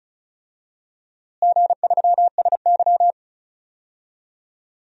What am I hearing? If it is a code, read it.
Morse code, heard as a keyed tone, about 36 wpm, 700 Hz. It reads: G3SY
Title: DXCC-100 Call Signs - 4 Characters - Repeated 2x - 35wpm